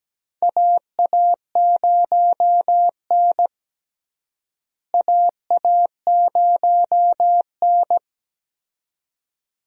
Morse code, heard as a keyed tone, about 17 wpm, 700 Hz. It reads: AA0N AA0N
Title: USA Call Signs - 4 Characters - Repeated 2x - 17wpm